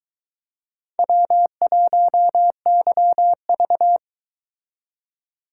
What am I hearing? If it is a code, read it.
Morse code, heard as a keyed tone, about 22 wpm, 700 Hz. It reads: W1YV